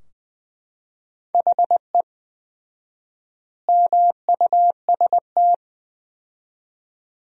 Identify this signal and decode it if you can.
Morse code, heard as a keyed tone, about 20 wpm, 700 Hz. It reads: HE MUST
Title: 4x Word-Spacing - Sentences from Top 500 Words - 20wpm